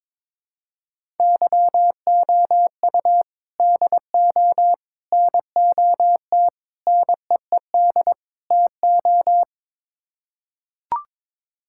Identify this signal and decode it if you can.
Morse code, heard as a keyed tone, about 22 wpm, 700 Hz. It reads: YOU DO NOT NEED TO
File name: practice / 22wpm / Build A Phrase - 5-Word Phrases - 22wpm